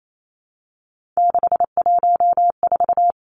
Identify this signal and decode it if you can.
Morse code, heard as a keyed tone, about 28 wpm, 700 Hz. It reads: 614
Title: Numbers - 3-Digit 28wpm